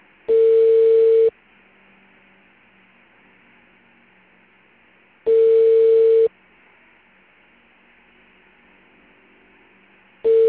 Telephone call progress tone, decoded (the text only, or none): ringback tone